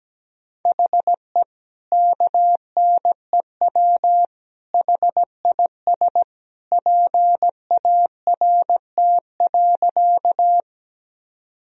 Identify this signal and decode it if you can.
Morse code, heard as a keyed tone, about 17 wpm, 700 Hz. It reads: HE KNEW HIS PART.
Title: Sentences from Top 1000 Words - Review Part 1 - 17wpm